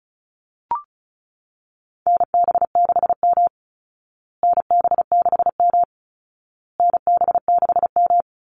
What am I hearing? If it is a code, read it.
Morse code, heard as a keyed tone, about 34 wpm, 700 Hz. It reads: NB6M NB6M NB6M
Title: Call Signs Repeated 2x - 35wpm